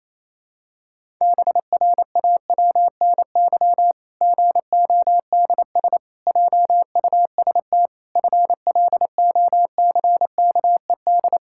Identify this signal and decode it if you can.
Morse code, heard as a keyed tone, about 27 wpm, 700 Hz. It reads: BRAWNY GODS JUST FLOCKED
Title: Pangram Challenge v2 - 28wpm